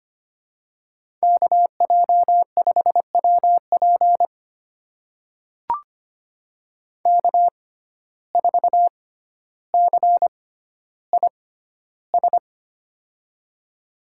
Morse code, heard as a keyed tone, about 25 wpm, spaced as for 13 wpm, 700 Hz. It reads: KJ5WP K4CIS
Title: USA Call Signs - 6x Character Spacing - 25wpm